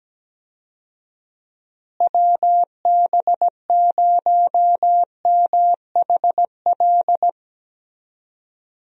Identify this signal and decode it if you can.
Morse code, heard as a keyed tone, about 17 wpm, 700 Hz. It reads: WB0MHL
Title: USA Call Signs - 6 Characters - Repeated 2x - 17wpm